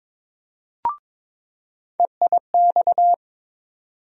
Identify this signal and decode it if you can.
Morse code, heard as a keyed tone, about 22 wpm, 700 Hz. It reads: EIX
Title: USA Call Sign Postfixes - 3 Letters 22wpm